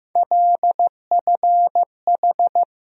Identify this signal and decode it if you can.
Morse code, heard as a keyed tone, about 15 wpm, 700 Hz. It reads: LFH
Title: USA Call Sign Postfixes - 3 Letters 15wpm